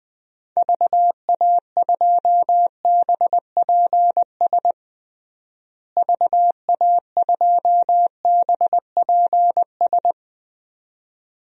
Morse code, heard as a keyed tone, about 20 wpm, 700 Hz. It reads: VA2BPS VA2BPS
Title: DXCC-100 Call Signs - 6 Characters - Repeated 2x - 20wpm